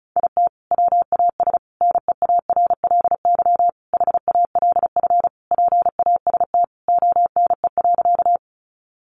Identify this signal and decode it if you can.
Morse code, heard as a keyed tone, about 34 wpm, 700 Hz. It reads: IT WAS NEARLY HALF PAST ONE.
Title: Sentences from Top 1300 Words 35wpm